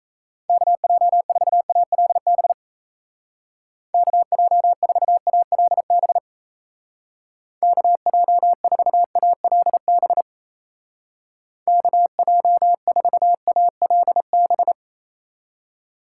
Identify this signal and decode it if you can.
Morse code, heard as a keyed tone, about 34 wpm, 700 Hz. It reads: KJ4ALB KJ4ALB KJ4ALB KJ4ALB
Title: Speed Racing - USA 2x3 Call Signs 28wpm